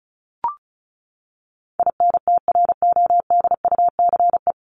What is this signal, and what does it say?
Morse code, heard as a keyed tone, about 34 wpm, 700 Hz. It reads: INTRODUCE
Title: ED Suffix - Top 100 Words with Base Words 35wpm